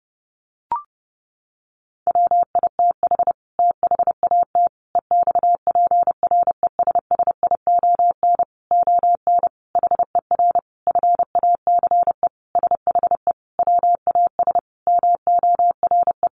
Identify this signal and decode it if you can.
Morse code, heard as a keyed tone, about 30 wpm, 700 Hz. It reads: WITH THAT EXPRESSION ON HER FACE SHE WAS MORE
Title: Sentences from Top 1200 Words 30wpm